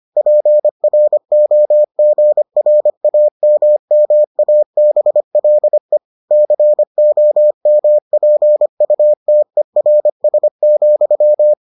Morse code, heard as a keyed tone, about 25 wpm, 600 Hz. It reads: PROGRAMMABLE COMPUTERS,